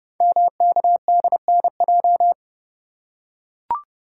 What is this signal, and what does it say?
Morse code, heard as a keyed tone, about 30 wpm, 700 Hz. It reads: MKDNJ
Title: Code Groups - 5 Letters - 30wpm